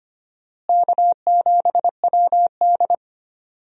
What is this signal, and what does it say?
Morse code, heard as a keyed tone, about 25 wpm, 700 Hz. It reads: K7WD
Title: USA Call Signs - 4 Characters - Repeated 3x - 25wpm